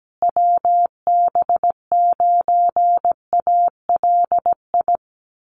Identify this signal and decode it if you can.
Morse code, heard as a keyed tone, about 17 wpm, 700 Hz. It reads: WB9ALI
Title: USA Call Signs Repeated 3x 17wpm